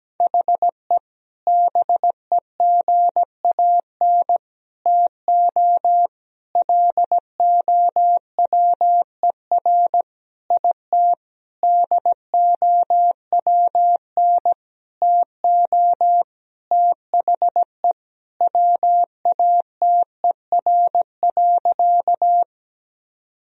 Morse code, heard as a keyed tone, about 17 wpm, 700 Hz. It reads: HE BEGAN TO LOWER IT DOWN TO THE WATER.
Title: Sentences from Top 1000 Words - Review Part 4 - 17wpm